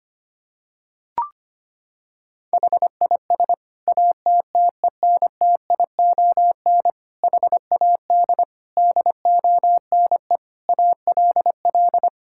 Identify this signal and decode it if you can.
Morse code, heard as a keyed tone, about 25 wpm, 700 Hz. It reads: HIS ATTENTION HAD DONE ALL
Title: Sentences from Top 1000 Words - Review Part 6 - 25wpm